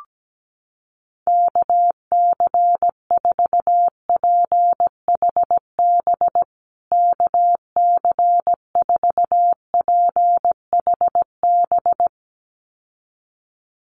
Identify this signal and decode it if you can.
Morse code, heard as a keyed tone, about 17 wpm, 700 Hz. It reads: KC4PHB KC4PHB
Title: USA Call Signs Repeated 2x 17wpm